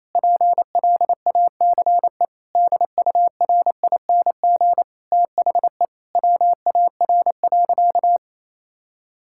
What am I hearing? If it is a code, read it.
Morse code, heard as a keyed tone, about 28 wpm, 700 Hz. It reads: PLACE DURING THE WAR.
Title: Sentences from Top 1000 Words - Review Part 6 - 28wpm